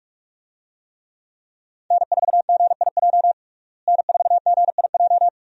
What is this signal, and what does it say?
Morse code, heard as a keyed tone, about 46 wpm, 700 Hz. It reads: N4GIJ N4GIJ
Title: USA Call Signs - 5 Characters - Repeated 2x - 45wpm